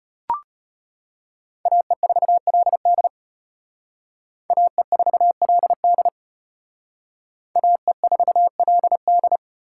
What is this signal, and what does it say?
Morse code, heard as a keyed tone, about 31 wpm, 700 Hz. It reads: AE4LD AE4LD AE4LD
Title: Speed Racing - USA Call Signs 25wpm